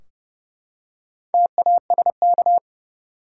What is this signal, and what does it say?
Morse code, heard as a keyed tone, about 29 wpm, 700 Hz. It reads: TASK
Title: Four Letter Words - 30wpm